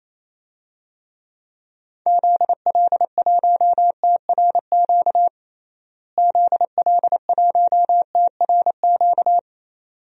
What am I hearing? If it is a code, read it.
Morse code, heard as a keyed tone, about 28 wpm, 700 Hz. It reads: ZL1TRQ ZL1TRQ
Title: DXCC-100 Call Signs - 6 Characters - Repeated 2x - 28wpm